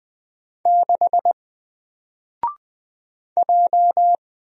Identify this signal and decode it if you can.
Morse code, heard as a keyed tone, about 20 wpm, 700 Hz. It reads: 6 J
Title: Single Letter-Number 20wpm